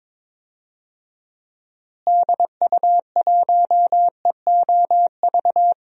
Morse code, heard as a keyed tone, about 22 wpm, 700 Hz. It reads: DU1EOV
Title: DXCC-100 Call Signs - 6 Characters - Repeated 2x - 22wpm